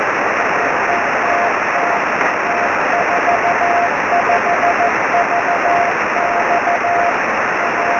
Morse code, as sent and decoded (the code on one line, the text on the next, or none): none